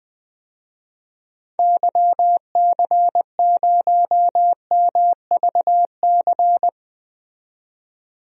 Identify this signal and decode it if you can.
Morse code, heard as a keyed tone, about 20 wpm, 700 Hz. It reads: YC0MVC
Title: DXCC-100 Call Signs - 6 Characters - 20wpm